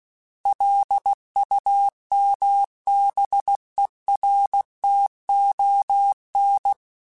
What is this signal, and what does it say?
Morse code, heard as a keyed tone, about 16 wpm, 800 Hz. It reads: LUMBERTON
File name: fromSMs / FromSMs mp3s / Cities@15